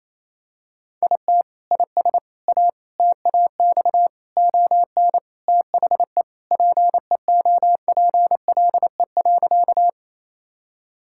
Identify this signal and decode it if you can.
Morse code, heard as a keyed tone, about 28 wpm, 700 Hz. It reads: IT IS A TAX ON THE PEOPLE.
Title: Sentences from Top 1000 Words - Review Part 5 - 28wpm